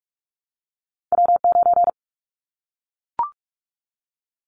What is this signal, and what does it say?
Morse code, heard as a keyed tone, about 44 wpm, 700 Hz. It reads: W9
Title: Callsign Prefixes for Top 25 DXCC - 45wpm